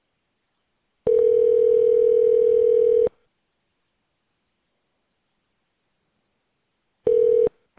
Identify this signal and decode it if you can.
Ringback tone (North America), 440 + 480 Hz.